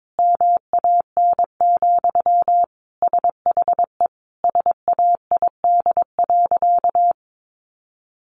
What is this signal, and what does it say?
Morse code, heard as a keyed tone, about 22 wpm, 700 Hz. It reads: MAN, SHE SAID.
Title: Sentences from Top 1100 Words 22wpm